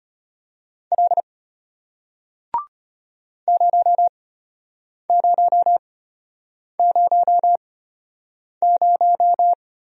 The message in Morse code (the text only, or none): L 0000